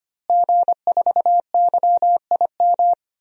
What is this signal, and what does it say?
Morse code, heard as a keyed tone, about 25 wpm, 700 Hz. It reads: G4YIM
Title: DXCC-100 Call Signs - 5 Characters - Repeated 3x - 25wpm